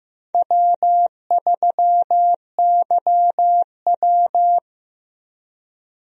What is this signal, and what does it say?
Morse code, heard as a keyed tone, about 15 wpm, 700 Hz. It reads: W3YW
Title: USA Call Signs - 4 Characters - Repeated 3x - 15wpm